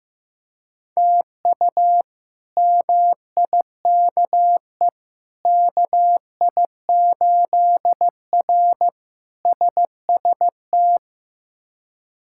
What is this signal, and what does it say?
Morse code, heard as a keyed tone, about 16 wpm, 700 Hz. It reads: TU MIKE KI8R SST
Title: K1USN SST QSO Practice - 15wpm